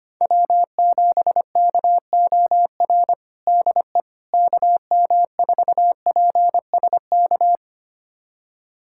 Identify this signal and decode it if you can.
Morse code, heard as a keyed tone, about 25 wpm, 700 Hz. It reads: W7KOR DE KM4PSK
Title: USA Callsign DE Callsign 25wpm